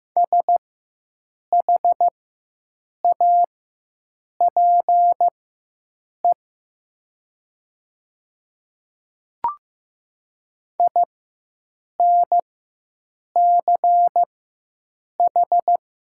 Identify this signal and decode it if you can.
Morse code, heard as a keyed tone, about 15 wpm, spaced as for 7 wpm, 700 Hz. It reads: SHAPE INCH
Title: 4x Character Spacing - Top 500 Words - 15wpm